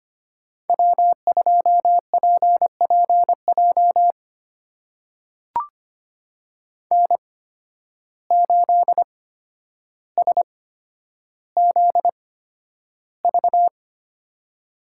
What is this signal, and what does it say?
Morse code, heard as a keyed tone, about 25 wpm, spaced as for 11 wpm, 700 Hz. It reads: W2PPJ N8SZV
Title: USA Call Signs - 5 Characters - 8x Character Spacing - 25wpm